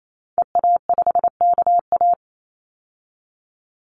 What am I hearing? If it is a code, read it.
Morse code, heard as a keyed tone, about 28 wpm, 700 Hz. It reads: EA5KA
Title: Call Signs Repeated 2x - 28wpm